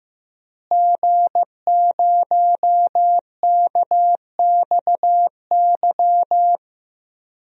Morse code, heard as a keyed tone, about 15 wpm, 700 Hz. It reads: G0KXY
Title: DXCC-100 Call Signs - 5 Characters - Repeated 2x - 15wpm